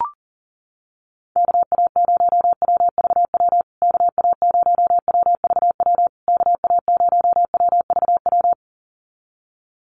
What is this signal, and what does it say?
Morse code, heard as a keyed tone, about 40 wpm, 700 Hz. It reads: KA0WVW KA0WVW KA0WVW
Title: USA Call Signs Repeated 3x 40wpm